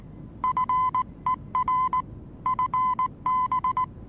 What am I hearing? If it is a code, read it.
Morse code, heard as a keyed tone, about 19 wpm, 1020 Hz. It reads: FER FB
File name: repos / Morsekurs / DJ4UF Morsen26